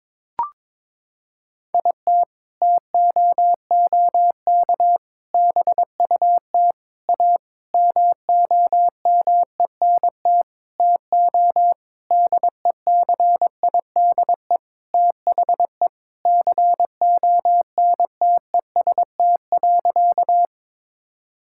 Morse code, heard as a keyed tone, about 22 wpm, 700 Hz. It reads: IT TOOK BUT A MOMENT TO DECIDE THE CONTEST.
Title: Sentences from Top 1600 Words 22wpm